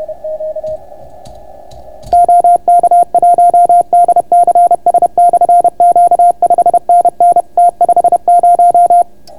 Morse code, heard as a keyed tone, about 31 wpm, 700 Hz. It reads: OK1DCS/Q 5NN T50